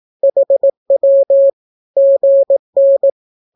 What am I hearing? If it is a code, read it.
Morse code, heard as a keyed tone, about 18 wpm, 550 Hz. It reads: HW GN
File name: naukapodstaw / skroty / skroty